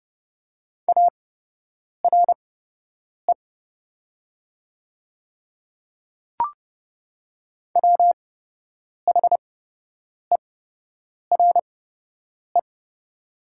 Morse code, heard as a keyed tone, about 30 wpm, spaced as for 8 wpm, 690 Hz. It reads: ARE WHERE
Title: 8x Character Spacing - Top 200 Words - 30wpm